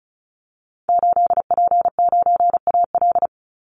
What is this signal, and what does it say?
Morse code, heard as a keyed tone, about 34 wpm, 700 Hz. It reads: 8P9AL
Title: Call Signs Repeated 2x - 35wpm